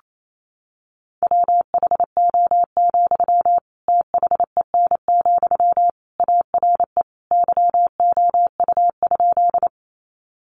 Morse code, heard as a keyed tone, about 28 wpm, 700 Hz. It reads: WHO, THEN, ARE YOU?
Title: Sentences from Top 100 Words 28wpm